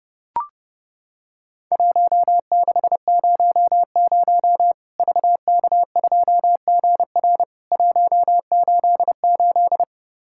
Morse code, heard as a keyed tone, about 30 wpm, 700 Hz. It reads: 1600 VK2GR 188
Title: Worked All Europe DX Contest - QTC Exchange - 30wpm